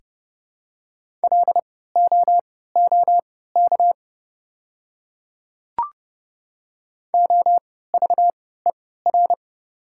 Morse code, heard as a keyed tone, about 30 wpm, spaced as for 17 wpm, 700 Hz. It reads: LOOK OVER
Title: 3x Character Spacing - Top 100 Words - 30wpm